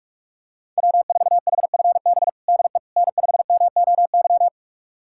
Morse code, heard as a keyed tone, about 46 wpm, 700 Hz. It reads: W4HFB DE N5MOY
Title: USA Callsign DE Callsign - 5 Characters - 45wpm